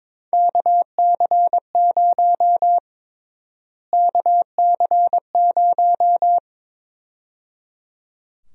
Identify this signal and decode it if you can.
Morse code, heard as a keyed tone, about 22 wpm, 700 Hz. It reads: KC0 KC0
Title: USA Call Sign Prefixes - Repeated 2x - 22wpm